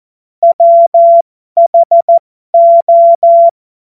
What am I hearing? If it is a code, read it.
Morse code, heard as a keyed tone, about 14 wpm, spaced as for 12 wpm, 680 Hz. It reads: WHO